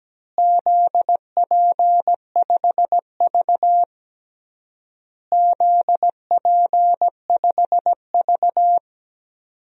Morse code, heard as a keyed tone, about 17 wpm, 700 Hz. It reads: ZP5V ZP5V
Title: DXCC-100 Call Signs - 4 Characters - Repeated 3x - 17wpm